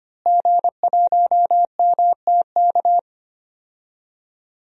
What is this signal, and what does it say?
Morse code, heard as a keyed tone, about 25 wpm, 700 Hz. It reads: G1MTK
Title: DXCC-100 Call Signs - 5 Characters - Repeated 2x - 25wpm